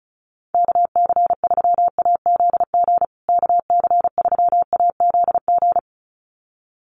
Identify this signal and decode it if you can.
Morse code, heard as a keyed tone, about 34 wpm, 700 Hz. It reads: KC3AZG KC3AZG
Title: USA Call Signs Repeated 2x 35wpm